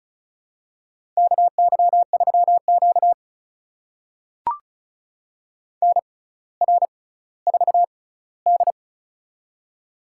Morse code, heard as a keyed tone, about 36 wpm, spaced as for 18 wpm, 700 Hz. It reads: KY3Q NR4D
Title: USA Call Signs - 4 Characters - 6x Character Spacing - 35wpm